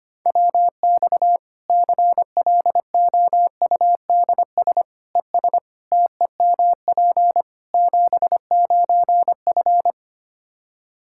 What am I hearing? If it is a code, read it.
Morse code, heard as a keyed tone, about 25 wpm, 700 Hz. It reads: WX CLOUDS ES TEMP 79F
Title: Practice QSO - Exchange 2 - Info - 25wpm